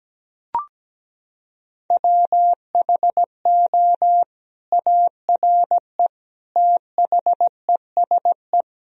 Morse code, heard as a keyed tone, about 17 wpm, 700 Hz. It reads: WHO ARE THESE
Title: Sentences from Top 1000 Words - Review Part 1 - 17wpm